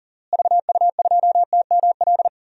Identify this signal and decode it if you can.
Morse code, heard as a keyed tone, about 40 wpm, 700 Hz. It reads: VU2TML
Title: DXCC-100 Call Signs - 6 Characters - Repeated 2x - 40wpm